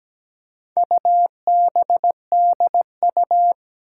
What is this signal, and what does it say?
Morse code, heard as a keyed tone, about 17 wpm, 700 Hz. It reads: UBDU
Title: Four Challenging Letter-Number Combinations 17wpm